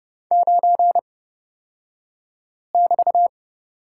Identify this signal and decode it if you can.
Morse code, heard as a keyed tone, about 31 wpm, spaced as for 5 wpm, 700 Hz. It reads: 9<BT>
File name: Lesson 36d - BT - Two Characters 30wpm